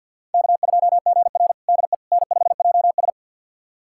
Morse code, heard as a keyed tone, about 53 wpm, 700 Hz. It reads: K2GR DE N5JS
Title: USA Callsign DE Callsign - 4 Characters - 50wpm